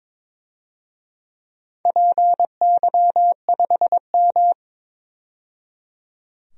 Morse code, heard as a keyed tone, about 22 wpm, 700 Hz. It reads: PY5M
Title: DXCC-100 Call Signs - 4 Characters - 22wpm